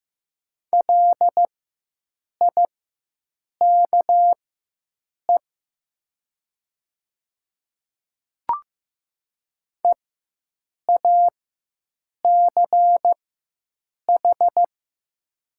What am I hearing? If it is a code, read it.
Morse code, heard as a keyed tone, about 15 wpm, spaced as for 7 wpm, 700 Hz. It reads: LIKE EACH
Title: 4x Character Spacing - Top 100 Words - 15wpm